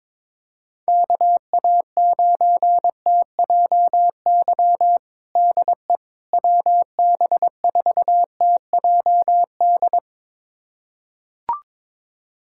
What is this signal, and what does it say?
Morse code, heard as a keyed tone, about 22 wpm, 700 Hz. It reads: KA9TJY DE WB4TJD
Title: USA Callsign DE Callsign 22wpm